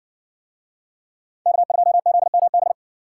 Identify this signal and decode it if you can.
Morse code, heard as a keyed tone, about 60 wpm, 700 Hz. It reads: K2ZMB